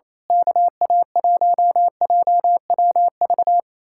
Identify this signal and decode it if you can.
Morse code, heard as a keyed tone, about 26 wpm, 700 Hz. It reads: KA1JWV
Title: Speed-Racing Common QSO Phrases - 28wpm